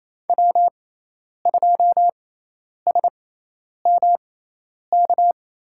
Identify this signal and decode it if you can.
Morse code, heard as a keyed tone, about 27 wpm, spaced as for 10 wpm, 700 Hz. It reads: W2SMK
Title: USA Call Signs - 5 Characters - 6x Character Spacing - 28wpm